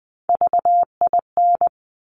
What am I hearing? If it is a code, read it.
Morse code, heard as a keyed tone, about 20 wpm, 700 Hz. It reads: VIN
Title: Top Three Letter Combinations in Words 20wpm